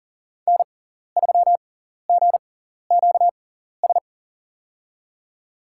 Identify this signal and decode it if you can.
Morse code, heard as a keyed tone, about 42 wpm, spaced as for 14 wpm, 700 Hz. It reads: N3GQS
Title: USA Call Signs - 5 Characters - 6x Character Spacing - 40wpm